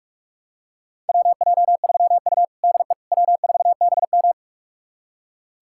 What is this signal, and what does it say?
Morse code, heard as a keyed tone, about 46 wpm, 700 Hz. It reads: WJ3U DE W4BM
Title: USA Callsign DE Callsign - 4 Characters - 45wpm